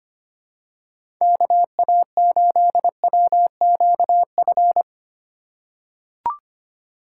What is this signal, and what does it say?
Morse code, heard as a keyed tone, about 25 wpm, 700 Hz. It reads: KA8WQF
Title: USA Call Signs - 6 Characters - 4x Character Spacing - 25wpm